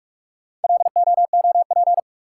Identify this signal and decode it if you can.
Morse code, heard as a keyed tone, about 46 wpm, 700 Hz. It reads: LOOP
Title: Top 180 QSO Elements 45wpm